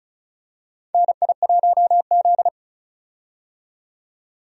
Morse code, heard as a keyed tone, about 36 wpm, 700 Hz. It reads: NI1Z
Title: USA Call Signs - 4 Characters - 35wpm